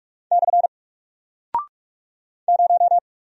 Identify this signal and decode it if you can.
Morse code, heard as a keyed tone, about 47 wpm, 700 Hz. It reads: / 0
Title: Single Character (Letter:Number:Punctuation:Prosign) Alt 45wpm